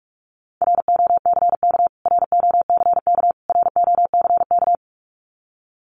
Morse code, heard as a keyed tone, about 44 wpm, 700 Hz. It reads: ROCK ROCK ROCK
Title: Top 500 Words Repeated 45wpm